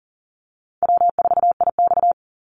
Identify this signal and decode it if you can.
Morse code, heard as a keyed tone, about 40 wpm, 700 Hz. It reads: W4IX
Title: Easy Call Signs - 40wpm